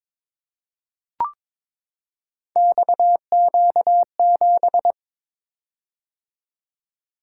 Morse code, heard as a keyed tone, about 22 wpm, 700 Hz. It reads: XQ7
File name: practice / 22wpm / Three Letter-Number Combinations - 22wpm